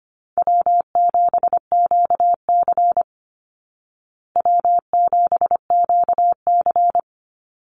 Morse code, heard as a keyed tone, about 25 wpm, 700 Hz. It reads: W7QC W7QC
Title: Easy Call Signs Repeated 2x - 25wpm